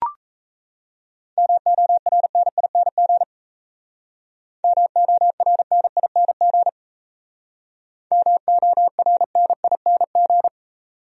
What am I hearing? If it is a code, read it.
Morse code, heard as a keyed tone, about 33 wpm, 700 Hz. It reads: MORNING MORNING MORNING
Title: Speed-Racing Top 500 Words 28wpm